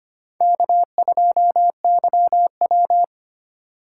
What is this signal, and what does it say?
Morse code, heard as a keyed tone, about 25 wpm, 700 Hz. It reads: K2YW
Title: USA Call Signs - 4 Characters - 3x Character Spacing - 25wpm